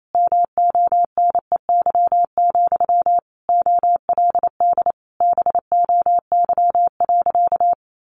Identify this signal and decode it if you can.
Morse code, heard as a keyed tone, about 28 wpm, 700 Hz. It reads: MONEY, OLD BOY.
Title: Sentences from Top 1500 Words 28wpm